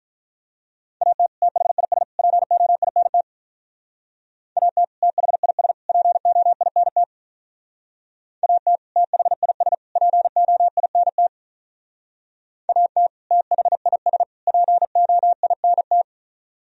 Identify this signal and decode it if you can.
Morse code, heard as a keyed tone, about 44 wpm, 700 Hz. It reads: AT THIS POINT AT THIS POINT AT THIS POINT AT THIS POINT